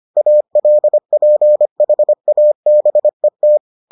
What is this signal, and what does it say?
Morse code, heard as a keyed tone, about 25 wpm, 600 Hz. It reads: ALPHABET